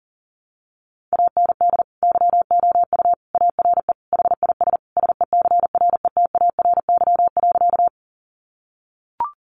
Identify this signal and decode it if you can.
Morse code, heard as a keyed tone, about 40 wpm, 700 Hz. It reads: AND YOU ARE HIS SECRETARY.
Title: Sentences from Top 1300 Words 40wpm